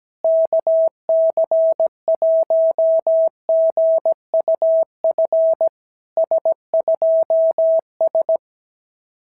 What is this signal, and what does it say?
Morse code, heard as a keyed tone, about 17 wpm, 650 Hz. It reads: KC1GUF S2S
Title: SOTA QSO Practice 17wpm